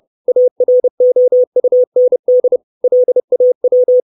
Morse code, heard as a keyed tone, about 30 wpm, 500 Hz. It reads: AROUND LAW
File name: 632 Words 30WPM